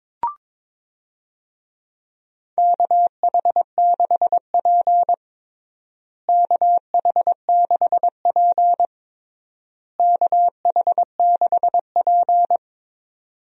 Morse code, heard as a keyed tone, about 22 wpm, 700 Hz. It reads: KH6P KH6P KH6P
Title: DXCC-100 Call Signs - 4 Characters - Repeated 3x - 22wpm